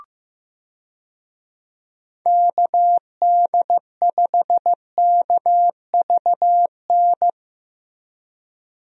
Morse code, heard as a keyed tone, about 15 wpm, 700 Hz. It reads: KD5KVN
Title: USA Call Signs - 6 Characters - Repeated 3x - 15wpm